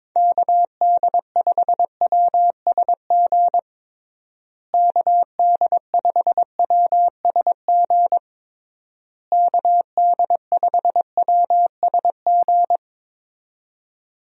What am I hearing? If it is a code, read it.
Morse code, heard as a keyed tone, about 22 wpm, 700 Hz. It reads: KD5WSG KD5WSG KD5WSG
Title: USA Call Signs - 6 Characters - Repeated 3x - 22wpm